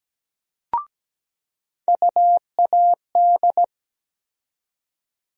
Morse code, heard as a keyed tone, about 17 wpm, 700 Hz. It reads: UAD
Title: USA Call Sign Postfixes - 3 Letters 17wpm